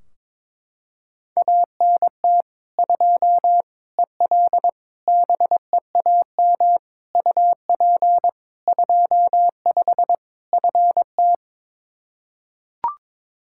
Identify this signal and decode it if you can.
Morse code, heard as a keyed tone, about 21 wpm, 700 Hz. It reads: ANT 2 EL BEAM UP 25 FT
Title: Practice QSO - Exchange 2 - Info - 22wpm